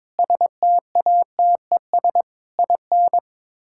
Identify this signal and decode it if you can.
Morse code, heard as a keyed tone, about 22 wpm, 700 Hz. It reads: STATES IN
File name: practice / 22wpm / Build A Phrase - 5-Word Phrases - 22wpm